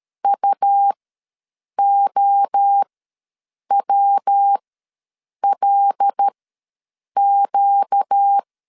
Morse code, heard as a keyed tone, about 13 wpm, spaced as for 7 wpm, 800 Hz. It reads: UOWLQ
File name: LEZIONI / 5 LBVC / 5 L ws 12wpm 2958